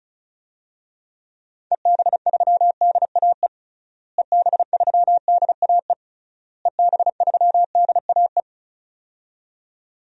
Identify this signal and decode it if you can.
Morse code, heard as a keyed tone, about 34 wpm, 700 Hz. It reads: EB3DAE EB3DAE EB3DAE
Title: DXCC-100 Call Signs - 6 Characters - Repeated 3x - 35wpm